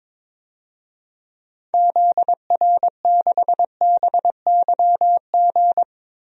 Morse code, heard as a keyed tone, about 22 wpm, 700 Hz. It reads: ZR6BYG
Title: DXCC-100 Call Signs - 6 Characters - Repeated 2x - 22wpm